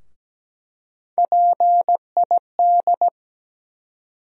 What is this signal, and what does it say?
Morse code, heard as a keyed tone, about 17 wpm, 700 Hz. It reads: PID
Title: USA Call Sign Postfixes - 3 Letters 17wpm